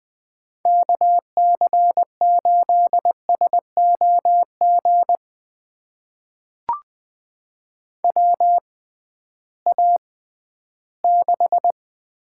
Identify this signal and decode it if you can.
Morse code, heard as a keyed tone, about 20 wpm, spaced as for 13 wpm, 700 Hz. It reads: KC8SOG WA6
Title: USA Call Signs - 6 Characters - 6x Character Spacing - 20wpm